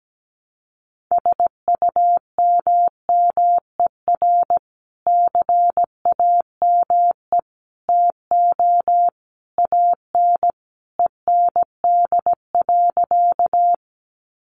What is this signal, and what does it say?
Morse code, heard as a keyed tone, about 17 wpm, 700 Hz. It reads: SUMMER CAME TO AN END.
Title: Sentences from Top 1100 Words 17wpm